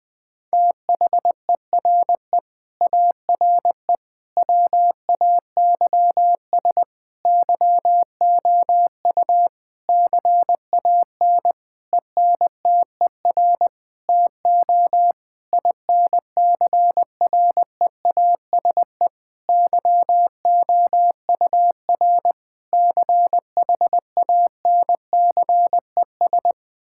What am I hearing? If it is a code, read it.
Morse code, heard as a keyed tone, about 21 wpm, 700 Hz. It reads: THERE ARE WAYS YOU CAN ENTER TO INCREASE YOUR CHANCES
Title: Sentences with 8-Word Phrases - 20wpm